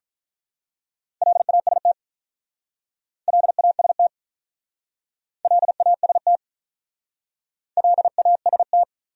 Morse code, heard as a keyed tone, about 45 wpm, 700 Hz. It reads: LAST LAST LAST LAST